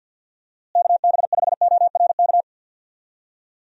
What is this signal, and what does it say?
Morse code, heard as a keyed tone, about 50 wpm, 700 Hz. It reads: KB5ORK
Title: USA Call Signs - 6 Characters - Repeated 2x - 50wpm